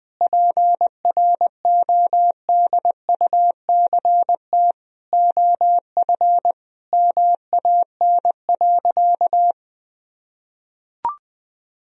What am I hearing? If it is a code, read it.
Morse code, heard as a keyed tone, about 21 wpm, 700 Hz. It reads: PRODUCT OF MAN.
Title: Sentences from Top 1000 Words - Review Part 2 - 20wpm